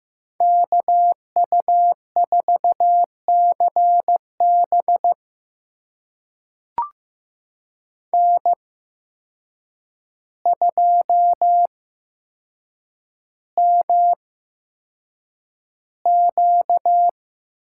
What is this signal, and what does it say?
Morse code, heard as a keyed tone, about 15 wpm, spaced as for 7 wpm, 700 Hz. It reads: KU4CB N2MQ
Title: USA Call Signs - 5 Characters - 8x Character Spacing - 15wpm